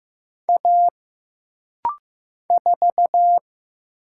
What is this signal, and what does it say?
Morse code, heard as a keyed tone, about 15 wpm, 700 Hz. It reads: A 4
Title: Single Letter-Number Alt - Rapid-Fire 15wpm